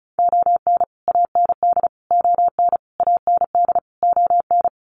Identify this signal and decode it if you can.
Morse code, heard as a keyed tone, about 34 wpm, 700 Hz. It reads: ON AND ON AND ON
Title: Sets of 5 Words Encore2 35wpm